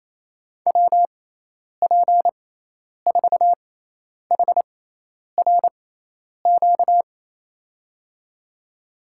Morse code, heard as a keyed tone, about 28 wpm, spaced as for 10 wpm, 700 Hz. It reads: WP4HRQ
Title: USA Call Signs - 6 Characters - 6x Character Spacing - 28wpm